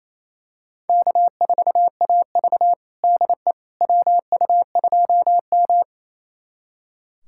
Morse code, heard as a keyed tone, about 28 wpm, 700 Hz. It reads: K4AV DE WU2M
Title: USA Callsign DE Callsign - 4 Characters - 28wpm